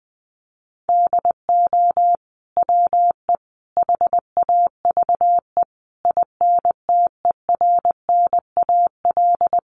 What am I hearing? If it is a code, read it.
Morse code, heard as a keyed tone, about 20 wpm, 700 Hz. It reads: DO WE HAVE INTERNAL
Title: Sentences from Top 1400 Words 20wpm